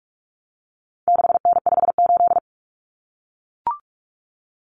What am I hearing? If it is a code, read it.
Morse code, heard as a keyed tone, about 44 wpm, 700 Hz. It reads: 6N58
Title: Common Cut Numbers - 4-Digit 45wpm